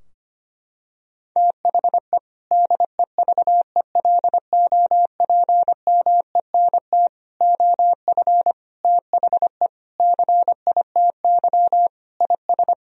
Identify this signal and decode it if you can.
Morse code, heard as a keyed tone, about 25 wpm, 700 Hz. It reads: THE DEVELOPMENT OF THE CITY IS
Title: Sentences from Top 1000 Words - Review Part 4 - 25wpm